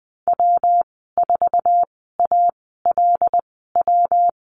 Morse code, heard as a keyed tone, about 20 wpm, spaced as for 14 wpm, 700 Hz. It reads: W4ALW
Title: 128 Lesson 21g - Number 6 - Call Signs 20wpm